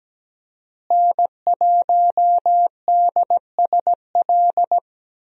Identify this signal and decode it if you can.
Morse code, heard as a keyed tone, about 17 wpm, 700 Hz. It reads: N1DSL
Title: USA Call Signs - 5 Characters - Repeated 2x - 17wpm